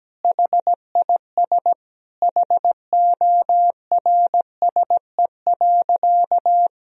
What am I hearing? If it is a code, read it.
Morse code, heard as a keyed tone, about 17 wpm, 700 Hz. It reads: HIS HORSE.
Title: Sentences from Top 1000 Words - Review Part 1 - 17wpm